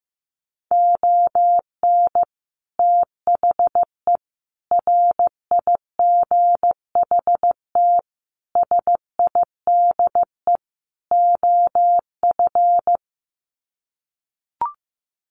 Morse code, heard as a keyed tone, about 15 wpm, 700 Hz. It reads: ON THE RIGHT SIDE OF
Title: Sets of 5 Words Encore 15wpm